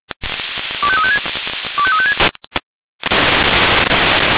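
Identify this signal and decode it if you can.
Five-tone selective call (ZVEI), heard as quick runs of five tones, tones of about 80 ms. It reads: 25356 25356
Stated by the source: ZVEI five-tone selective call received on an SDR.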